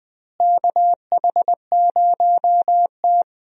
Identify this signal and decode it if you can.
Morse code, heard as a keyed tone, about 20 wpm, 700 Hz. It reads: KH0T
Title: USA Call Signs - 4 Characters - Repeated 3x - 20wpm